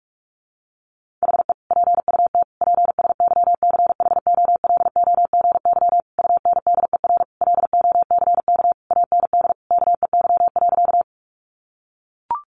Morse code, heard as a keyed tone, about 44 wpm, 700 Hz. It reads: HE PUT PSYCHOLOGY UNDER LOCK AND KEY.
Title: Sentences from Top 1400 Words 45wpm